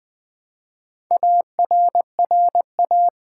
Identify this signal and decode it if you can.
Morse code, heard as a keyed tone, about 20 wpm, 700 Hz. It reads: ARRA
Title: Build A Word - 1 Letter At A Time - 5-Letter Words - 20wpm